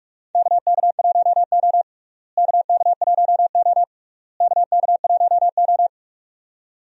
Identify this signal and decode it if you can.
Morse code, heard as a keyed tone, about 42 wpm, 700 Hz. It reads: KK1O KK1O KK1O
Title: USA Call Signs - 4 Characters - Repeated 3x - 45wpm